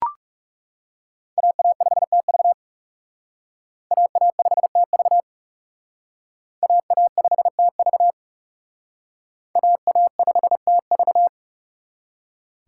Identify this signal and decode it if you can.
Morse code, heard as a keyed tone, about 37 wpm, 700 Hz. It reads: AA5TV AA5TV AA5TV AA5TV
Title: Speed Racing - USA 2x2 Call Signs 30wpm